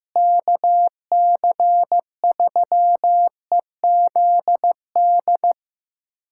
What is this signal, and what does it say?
Morse code, heard as a keyed tone, about 16 wpm, 700 Hz. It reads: KC3EZD
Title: USA Callsign DE Callsign 15wpm